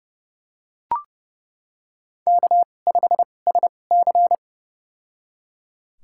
Morse code, heard as a keyed tone, about 32 wpm, spaced as for 22 wpm, 700 Hz. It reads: K5SC
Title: USA Call Signs - 4 Characters - 2x Character Spacing - 30wpm